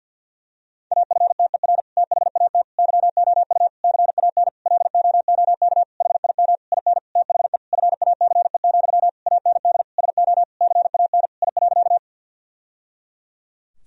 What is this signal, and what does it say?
Morse code, heard as a keyed tone, about 50 wpm, 700 Hz. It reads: AFTER THAT YOU CAN LOOK HIM IN THE FACE, AND SO CAN I.
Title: Sentences from Top 1000 Words - Review Part 2 - 50wpm